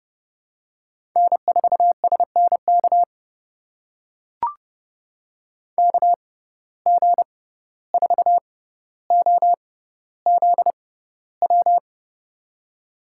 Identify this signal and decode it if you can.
Morse code, heard as a keyed tone, about 31 wpm, spaced as for 15 wpm, 700 Hz. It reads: N4SNK KG4OZW
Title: USA Call Signs - 6x Character Spacing - 30wpm